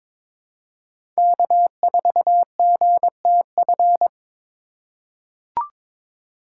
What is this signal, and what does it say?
Morse code, heard as a keyed tone, about 22 wpm, 700 Hz. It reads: K4GTF
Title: USA Call Signs - 5 Characters - 4x Character Spacing - 22wpm